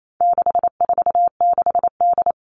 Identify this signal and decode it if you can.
Morse code, heard as a keyed tone, about 28 wpm, 700 Hz. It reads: 646D
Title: All Cut Numbers - 4-Digit 28wpm